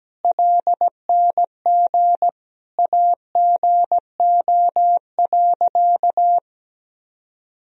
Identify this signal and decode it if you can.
Morse code, heard as a keyed tone, about 17 wpm, 700 Hz. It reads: LNG AGO.
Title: Sentences from Top 500 Words and Abbreviations 17wpm